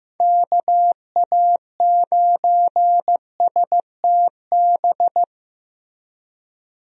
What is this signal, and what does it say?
Morse code, heard as a keyed tone, about 15 wpm, 700 Hz. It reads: KA9STB
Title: USA Call Signs - 6 Characters - 15wpm